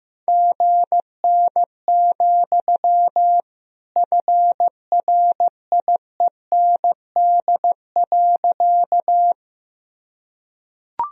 Morse code, heard as a keyed tone, about 16 wpm, 700 Hz. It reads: GN, FRIEND.
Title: Sentences from Top 300 Words and Abbreviations 15wpm